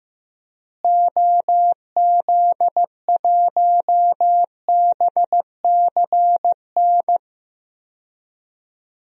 Morse code, heard as a keyed tone, about 15 wpm, 700 Hz. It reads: OZ1BCN
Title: DXCC-100 Call Signs - 6 Characters - Repeated 3x - 15wpm